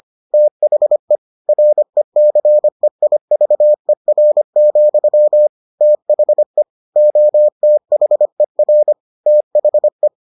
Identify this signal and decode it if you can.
Morse code, heard as a keyed tone, about 25 wpm, 600 Hz. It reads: THE RECEIVER, THE OTHER THE